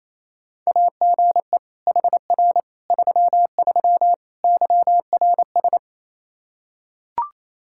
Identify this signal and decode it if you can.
Morse code, heard as a keyed tone, about 27 wpm, 700 Hz. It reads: AGE HR 33 YRS
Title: K1USN SST QSO Practice 28wpm